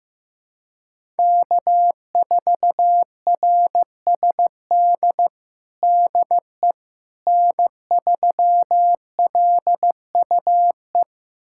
Morse code, heard as a keyed tone, about 15 wpm, 700 Hz. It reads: K4RSD DE N3LUE
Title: USA Callsign DE Callsign - 5 Characters - 15wpm